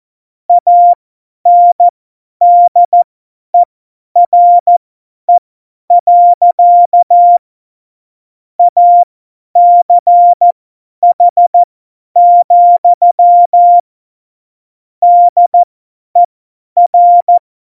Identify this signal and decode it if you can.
Morse code, heard as a keyed tone, about 14 wpm, spaced as for 10 wpm, 700 Hz. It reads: ANDERE. ACH, DER
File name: Morsekurs / Witze / Witz 17.Juli 2018, 50 F70